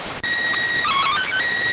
SSTV (slow-scan television), PD 290.